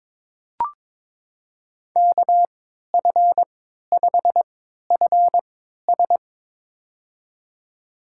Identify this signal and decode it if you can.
Morse code, heard as a keyed tone, about 22 wpm, spaced as for 12 wpm, 700 Hz. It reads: KF5FS
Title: USA Call Signs - 5 Characters - 3x Character Spacing - 22wpm